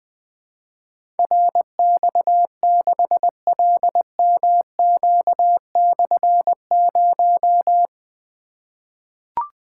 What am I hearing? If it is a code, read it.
Morse code, heard as a keyed tone, about 20 wpm, 700 Hz. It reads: RX6LMQ/0
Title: Difficult Call Signs 20wpm